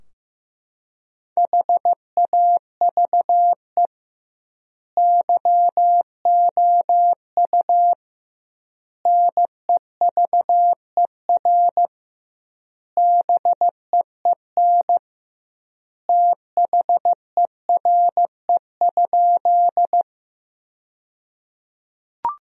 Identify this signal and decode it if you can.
Morse code, heard as a keyed tone, about 15 wpm, 700 Hz. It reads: HAVE YOU NEVER BEEN THERE?
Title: 2x Word-Spacing - Sentences from Top 300 Words - 15wpm